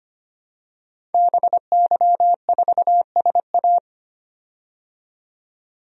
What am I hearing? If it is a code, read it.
Morse code, heard as a keyed tone, about 25 wpm, 700 Hz. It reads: BY4SA
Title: DXCC-100 Call Signs - 5 Characters - 25wpm